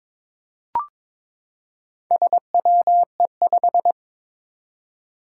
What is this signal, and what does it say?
Morse code, heard as a keyed tone, about 22 wpm, 700 Hz. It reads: SWE5
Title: Four Challenging Letter-Number Combinations 22wpm